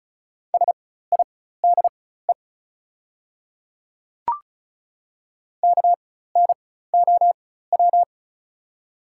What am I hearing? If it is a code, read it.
Morse code, heard as a keyed tone, about 36 wpm, spaced as for 16 wpm, 700 Hz. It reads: SIDE KNOW
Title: 4x Character Spacing - Top 100 Words - 35wpm